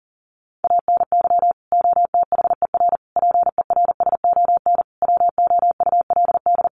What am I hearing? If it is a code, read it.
Morse code, heard as a keyed tone, about 40 wpm, 700 Hz. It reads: ANY OTHER PERSON WOULD
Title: Sentences from Top 1100 Words 40wpm